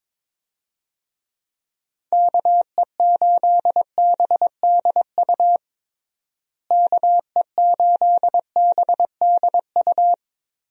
Morse code, heard as a keyed tone, about 22 wpm, 700 Hz. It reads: KE8BDU KE8BDU
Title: USA Call Signs - 6 Characters - Repeated 3x - 22wpm